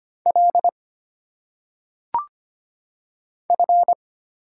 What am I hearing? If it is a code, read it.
Morse code, heard as a keyed tone, about 25 wpm, 700 Hz. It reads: L F
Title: Single Letters - Phonetic Pronunciation 25wpm